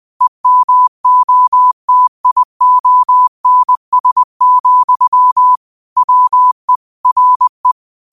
Morse code, heard as a keyed tone, about 20 wpm, 1000 Hz. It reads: EMOTIONS, WE RE
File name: qotdfiles / 20wpm